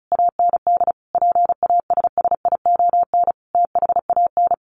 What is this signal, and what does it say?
Morse code, heard as a keyed tone, about 34 wpm, 700 Hz. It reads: AND PASSION THAN
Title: Sentences from Top 1300 Words 35wpm